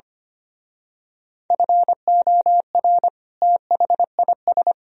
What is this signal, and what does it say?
Morse code, heard as a keyed tone, about 29 wpm, 700 Hz. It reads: FOR THIS
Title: Speed-Racing - Sets of 2 Words 25wpm